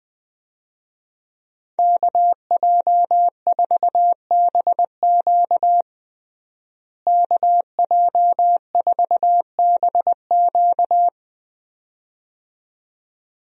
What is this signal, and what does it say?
Morse code, heard as a keyed tone, about 20 wpm, 700 Hz. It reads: KJ4BQ KJ4BQ
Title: USA Call Signs - 5 Characters - Repeated 2x - 20wpm